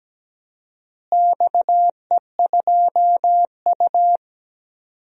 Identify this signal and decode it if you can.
Morse code, heard as a keyed tone, about 17 wpm, 700 Hz. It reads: XE2U
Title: DXCC-100 Call Signs - 4 Characters - Repeated 3x - 17wpm